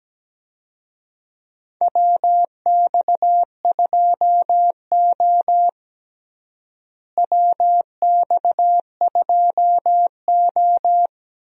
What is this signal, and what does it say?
Morse code, heard as a keyed tone, about 17 wpm, 700 Hz. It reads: WX2O WX2O
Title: USA Call Signs - 4 Characters - Repeated 2x - 17wpm